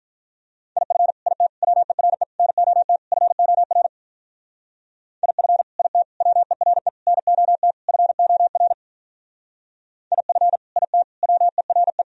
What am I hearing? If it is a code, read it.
Morse code, heard as a keyed tone, about 42 wpm, 700 Hz. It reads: IF IT WERE NOT FOR IF IT WERE NOT FOR IF IT WERE
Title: Speed-Racing - Sets of 5 Words 35wpm